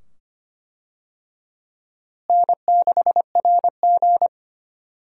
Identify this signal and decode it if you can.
Morse code, heard as a keyed tone, about 25 wpm, 700 Hz. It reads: N6RG
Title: USA Call Signs - 4 Characters - Repeated 2x - 25wpm